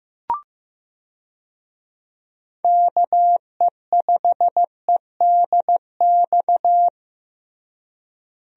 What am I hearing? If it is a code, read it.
Morse code, heard as a keyed tone, about 15 wpm, 700 Hz. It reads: KE5EDX
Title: USA Call Signs - 6 Characters - Repeated 2x - 15wpm